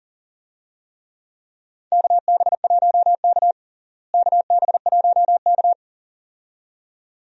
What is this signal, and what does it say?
Morse code, heard as a keyed tone, about 40 wpm, 700 Hz. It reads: KB1K KB1K
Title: USA Call Signs - 4 Characters - Repeated 2x - 40wpm